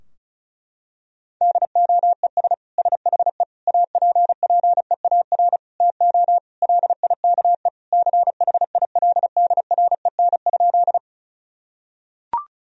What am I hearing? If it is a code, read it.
Morse code, heard as a keyed tone, about 35 wpm, 700 Hz. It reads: DOES SHE APPEAR TO LIKE CHILDREN?
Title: Sentences from Top 1000 Words - Review Part 2 - 35wpm